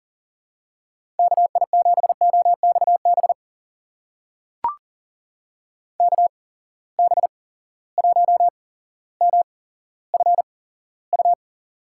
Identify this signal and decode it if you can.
Morse code, heard as a keyed tone, about 40 wpm, spaced as for 17 wpm, 700 Hz. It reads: KI7OXB KB1MFU
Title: USA Call Signs - 6 Characters - 8x Character Spacing - 40wpm